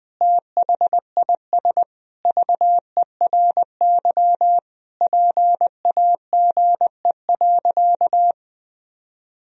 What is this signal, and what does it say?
Morse code, heard as a keyed tone, about 20 wpm, 700 Hz. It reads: THIS VERY PAGE.
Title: Sentences from Top 1000 Words - Review Part 1 - 20wpm